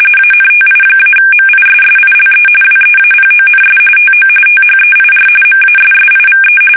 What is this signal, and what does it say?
RTTY, about 45.45 baud. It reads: ANTI-MISSILE WEAPONS TO THEIR MASSIVE
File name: fsk850 / battleshipnj